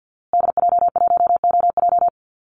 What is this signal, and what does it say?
Morse code, heard as a keyed tone, about 50 wpm, 700 Hz. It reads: DJ1OJ
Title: Call Signs Repeated 2x - 50wpm